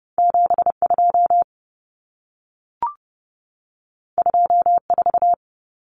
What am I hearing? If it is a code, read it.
Morse code, heard as a keyed tone, about 30 wpm, 700 Hz. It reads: 72 24
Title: Common Cut Numbers - 2-Digit 30wpm